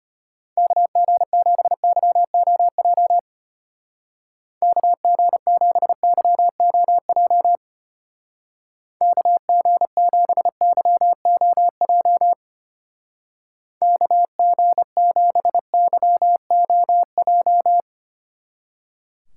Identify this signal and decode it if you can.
Morse code, heard as a keyed tone, about 31 wpm, 700 Hz. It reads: KG7YOJ KG7YOJ KG7YOJ KG7YOJ